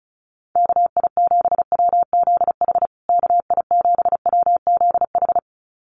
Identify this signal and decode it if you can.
Morse code, heard as a keyed tone, about 34 wpm, 700 Hz. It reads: KI7WZH KI7WZH
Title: USA Call Signs Repeated 2x 35wpm